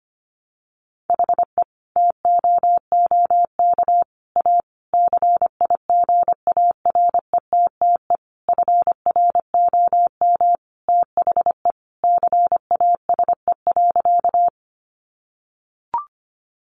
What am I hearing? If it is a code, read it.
Morse code, heard as a keyed tone, about 25 wpm, 700 Hz. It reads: HE TOOK A CIGARETTE FROM THE CASE.
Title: Sentences from Top 1000 Words 25wpm